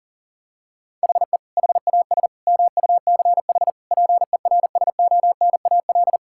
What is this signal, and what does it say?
Morse code, heard as a keyed tone, about 40 wpm, 700 Hz. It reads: HE HAS MUCH PERSONAL
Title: Sentences from Top 1000 Words - Review Part 5 - 40wpm